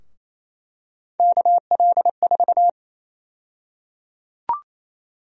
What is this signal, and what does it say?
Morse code, heard as a keyed tone, about 28 wpm, 700 Hz. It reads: KL4
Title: USA Call Sign Prefixes - 28wpm